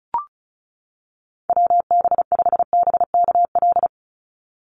Morse code, heard as a keyed tone, about 34 wpm, 700 Hz. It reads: WB5BKL
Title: Call Signs - 35wpm